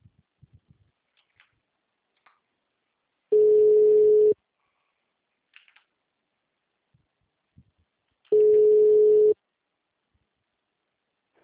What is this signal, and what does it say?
Ringback tone (Europe, 425 Hz).
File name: out- -2011- - - .24394